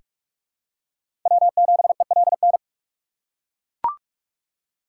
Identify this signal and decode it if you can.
Morse code, heard as a keyed tone, about 43 wpm, 700 Hz. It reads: W7ELN
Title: USA Call Signs - 3x Character Spacing - 45wpm